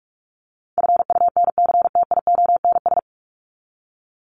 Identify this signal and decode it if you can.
Morse code, heard as a keyed tone, about 44 wpm, 700 Hz. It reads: FUNCTIONS
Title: ES:S Suffix - Top 100 Words with Base Words 45wpm